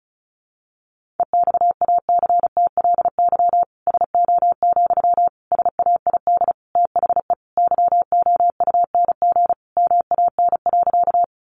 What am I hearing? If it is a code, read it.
Morse code, heard as a keyed tone, about 34 wpm, 700 Hz. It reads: EXACTLY SO, SAID THE YOUNG MAN.
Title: Sentences from Top 900 Words 35wpm